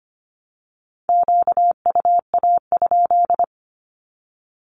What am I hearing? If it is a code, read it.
Morse code, heard as a keyed tone, about 25 wpm, 700 Hz. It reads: QUA?
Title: Q-Codes 25wpm